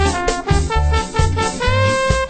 There is no code.